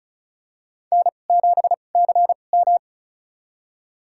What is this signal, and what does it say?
Morse code, heard as a keyed tone, about 36 wpm, spaced as for 25 wpm, 700 Hz. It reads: N7CM
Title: USA Call Signs - 4 Characters - 2x Character Spacing - 35wpm